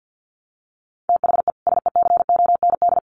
Morse code, heard as a keyed tone, about 50 wpm, 700 Hz. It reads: THE SECOND
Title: Sets of 2 Words 50wpm